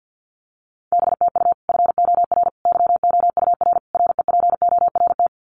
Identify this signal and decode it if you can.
Morse code, heard as a keyed tone, about 48 wpm, 700 Hz. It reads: BTU FOR YOUR REPORT
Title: Common QSO Phrases 50wpm